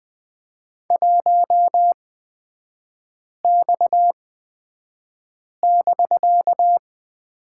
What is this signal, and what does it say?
Morse code, heard as a keyed tone, about 20 wpm, spaced as for 5 wpm, 700 Hz. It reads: 1X<BK>
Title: 250 Lesson 40e - BK - Three Characters 20wpm